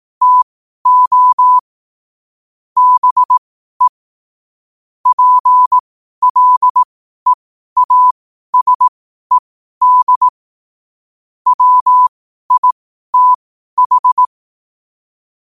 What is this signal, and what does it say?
Morse code, heard as a keyed tone, about 18 wpm, spaced as for 12 wpm, 1000 Hz. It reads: TO BE PLEASED WITH
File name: qotdfiles / 12wpm